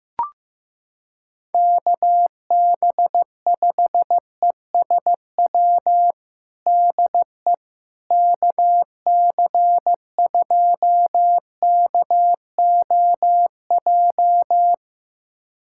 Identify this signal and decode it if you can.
Morse code, heard as a keyed tone, about 15 wpm, 700 Hz. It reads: KB5ESW DE KC2KOJ
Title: USA Callsign DE Callsign 15wpm